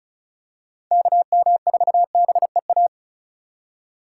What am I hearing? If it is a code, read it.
Morse code, heard as a keyed tone, about 34 wpm, 700 Hz. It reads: KM4BEA
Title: USA Call Signs - 6 Characters - 2x Character Spacing - 35wpm